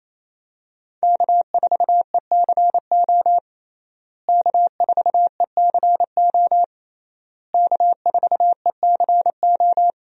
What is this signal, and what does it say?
Morse code, heard as a keyed tone, about 28 wpm, 700 Hz. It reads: K4ECO K4ECO K4ECO
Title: USA Call Signs - 5 Characters - Repeated 3x - 28wpm